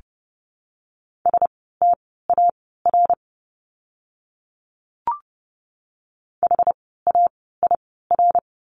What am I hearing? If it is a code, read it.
Morse code, heard as a keyed tone, about 30 wpm, spaced as for 17 wpm, 700 Hz. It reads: STAR HAIR
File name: Lesson 17f - Letter u - Words 30wpm